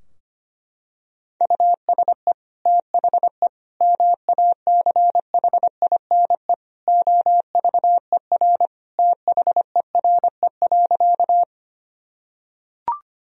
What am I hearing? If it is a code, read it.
Morse code, heard as a keyed tone, about 25 wpm, 700 Hz. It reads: USE THE MACHINE OVER THERE.
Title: Sentences from Top 1000 Words - Review Part 2 - 25wpm